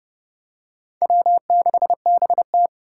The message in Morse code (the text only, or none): W6BT